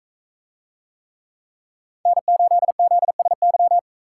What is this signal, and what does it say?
Morse code, heard as a keyed tone, about 41 wpm, 700 Hz. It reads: N8ZSY